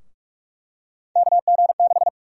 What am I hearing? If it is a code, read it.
Morse code, heard as a keyed tone, about 41 wpm, 700 Hz. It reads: KG6
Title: USA Call Sign Prefixes - 45wpm